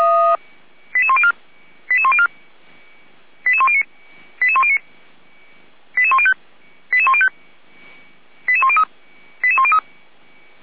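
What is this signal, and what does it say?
Five-tone selective call (ZVEI), heard as quick runs of five tones, tones of about 70 ms. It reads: 88104 88104 88109 88109 88105 88105 88103 88103
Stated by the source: ZVEI five-tone call received on SDR by feuerwehr-orlamuende (fire brigade Orlamünde).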